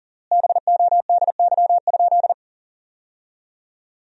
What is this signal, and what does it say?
Morse code, heard as a keyed tone, about 40 wpm, 700 Hz. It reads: BODY?
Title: 3x Word-Spacing - Sentences from Top 400 Words - 40wpm